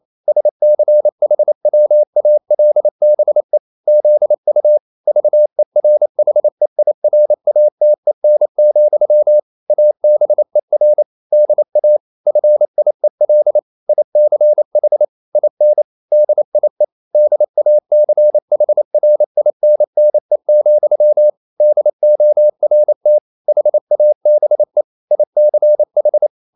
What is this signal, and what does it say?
Morse code, heard as a keyed tone, about 28 wpm, 600 Hz. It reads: SCHWALBE ZU VERHEIRATEN, ABER DA FIEL ICH IN DIE DACHRINNE, DORT HABE ICH
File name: Liebespaar4 28wpm